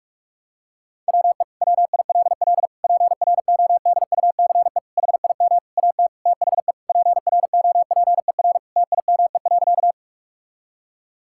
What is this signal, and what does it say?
Morse code, heard as a keyed tone, about 46 wpm, 700 Hz. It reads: WE WILL PRODUCE HIM AT THE PROPER TIME.
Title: Sentences from Top 1000 Words - Review Part 5 - 45wpm